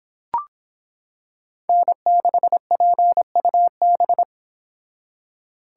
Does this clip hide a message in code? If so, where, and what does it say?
Yes: Morse code from 1.7 to 4.2 s, heard as a keyed tone, about 26 wpm, 700 Hz. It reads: N6PUB